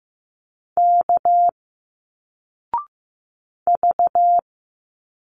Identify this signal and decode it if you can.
Morse code, heard as a keyed tone, about 15 wpm, 700 Hz. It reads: K V
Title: Single Letters 15wpm